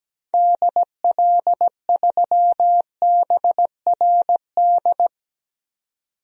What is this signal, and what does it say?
Morse code, heard as a keyed tone, about 17 wpm, 700 Hz. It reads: DL3BRD
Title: DXCC-100 Call Signs - 6 Characters - 17wpm